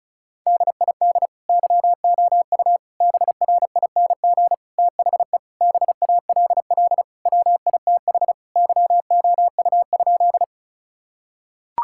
Morse code, heard as a keyed tone, about 36 wpm, 700 Hz. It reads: DID YOU BRING THE BALL WITH YOU?
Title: Sentences from Top 1000 Words - Review Part 6 - 35wpm